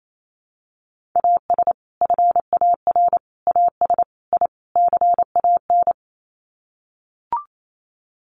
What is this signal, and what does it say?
Morse code, heard as a keyed tone, about 28 wpm, 700 Hz. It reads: AS FAR AS I CAN
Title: Sets of 5 Words Encore 28wpm